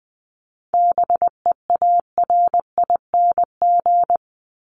Morse code, heard as a keyed tone, about 20 wpm, 700 Hz. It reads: BEARING
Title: ING Suffix - Top 100 Words with Base Words 20wpm